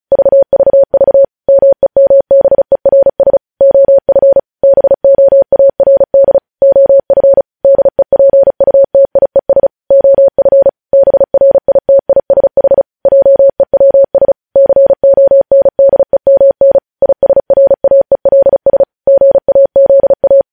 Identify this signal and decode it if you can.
Morse code, heard as a keyed tone, about 36 wpm, 550 Hz. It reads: VVV MEMBERS OF BOARD OF DEPUTIES OF BRITISH JEWS CONDEMN ISRAELS GAZA
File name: podcast / News headlines-1x-35x35